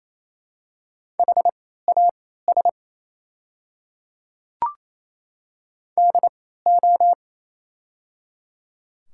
Morse code, heard as a keyed tone, about 27 wpm, spaced as for 16 wpm, 700 Hz. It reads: HAS DO
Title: 3x Character Spacing - Top 200 Words - 28wpm